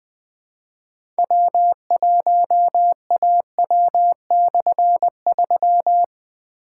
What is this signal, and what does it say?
Morse code, heard as a keyed tone, about 21 wpm, 700 Hz. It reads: W1AW/3
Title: Difficult Call Signs 20wpm